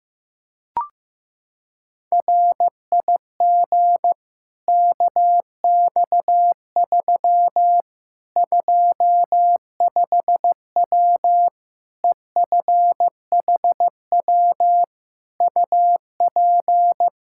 Morse code, heard as a keyed tone, about 15 wpm, 700 Hz. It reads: RIG KX3 25W EFHW UP
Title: Practice QSO - Exchange 2 - Info - 15wpm